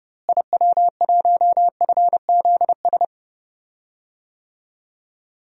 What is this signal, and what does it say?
Morse code, heard as a keyed tone, about 29 wpm, 700 Hz. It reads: IW1FZS
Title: DXCC-100 Call Signs - 6 Characters - Repeated 2x - 30wpm